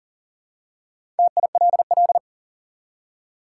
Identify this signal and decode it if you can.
Morse code, heard as a keyed tone, about 38 wpm, 700 Hz. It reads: TILL
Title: Four Letter Words - 40wpm